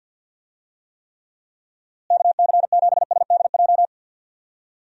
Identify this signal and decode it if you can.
Morse code, heard as a keyed tone, about 50 wpm, 700 Hz. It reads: KC7SDJ
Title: USA Call Signs - 6 Characters - Repeated 3x - 50wpm